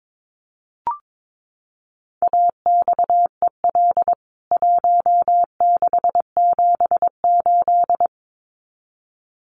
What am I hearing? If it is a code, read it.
Morse code, heard as a keyed tone, about 22 wpm, 700 Hz. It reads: AXEL 1678
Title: CWT Exchange 22wpm